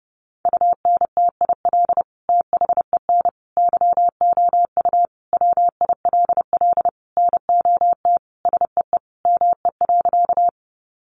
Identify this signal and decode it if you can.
Morse code, heard as a keyed tone, about 30 wpm, 700 Hz. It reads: UNTIL THEN YOU WILL NOT SEE ME.
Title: Sentences from Top 600 Words 30wpm